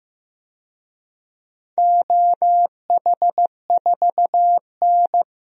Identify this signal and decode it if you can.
Morse code, heard as a keyed tone, about 15 wpm, 700 Hz. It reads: OH4N
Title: DXCC-100 Call Signs - 4 Characters - 15wpm